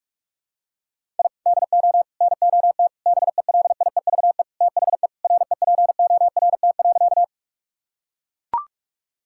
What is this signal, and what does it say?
Morse code, heard as a keyed tone, about 42 wpm, 690 Hz. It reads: I DO NOT BELIEVE THE REPORT.
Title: Sentences from Top 1000 Words - Review Part 6 - 45wpm